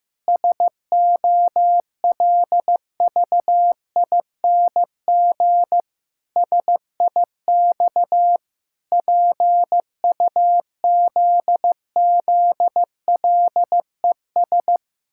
Morse code, heard as a keyed tone, about 15 wpm, 700 Hz. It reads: SOLVING SIX PUZZLES
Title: Pangram Challenge v2 - 15wpm